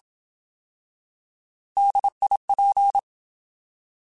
Morse code, heard as a keyed tone, about 27 wpm, 800 Hz. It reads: DIP